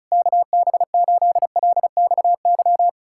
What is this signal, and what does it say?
Morse code, heard as a keyed tone, about 40 wpm, 700 Hz. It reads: KB8LXY